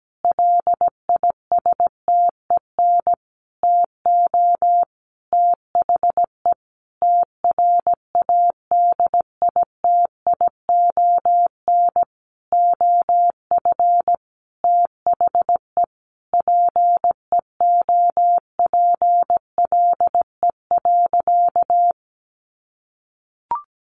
Morse code, heard as a keyed tone, about 17 wpm, 700 Hz. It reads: LISTEN TO THE TRADITION OF THE PEOPLE.
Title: Sentences from Top 1500 Words 17wpm